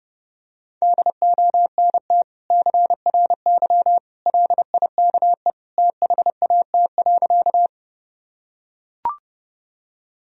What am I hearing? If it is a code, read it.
Morse code, heard as a keyed tone, about 30 wpm, 700 Hz. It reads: DONT CRY LIKE THAT.
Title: Sentences from Top 1000 Words - Review Part 2 - 30wpm